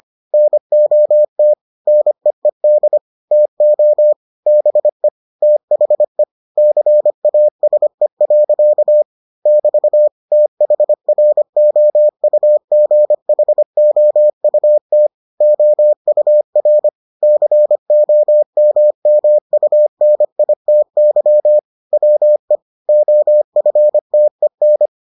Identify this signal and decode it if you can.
Morse code, heard as a keyed tone, about 25 wpm, 600 Hz. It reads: NOT NEED TO BE THE CASE. <BT> THROUGHOUT OUR COMMUNITY WE OFTEN